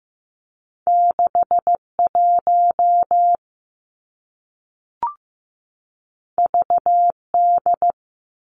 Morse code, heard as a keyed tone, about 15 wpm, 700 Hz. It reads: 6 1 V D
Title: Two Challenging Letter-Number Combinations 15wpm